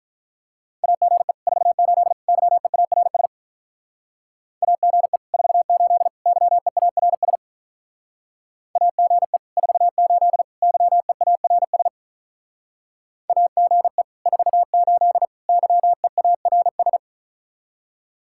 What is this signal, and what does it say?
Morse code, heard as a keyed tone, about 44 wpm, 700 Hz. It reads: AGE 48 YEARS AGE 48 YEARS AGE 48 YEARS AGE 48 YEARS